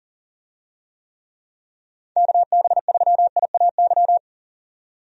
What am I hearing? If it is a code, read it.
Morse code, heard as a keyed tone, about 40 wpm, 700 Hz. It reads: KB3IAY
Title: USA Call Signs - 6 Characters - 40wpm